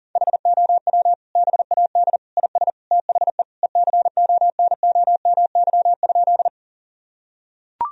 Morse code, heard as a keyed tone, about 40 wpm, 700 Hz. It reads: HOW BAD IS THE ECONOMY?
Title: Sentences from Top 1000 Words - Review Part 3 - 40wpm